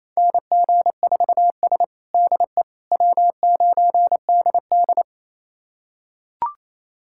Morse code, heard as a keyed tone, about 28 wpm, 700 Hz. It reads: NG4S DE W9DD
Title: USA Callsign DE Callsign - 4 Characters - 28wpm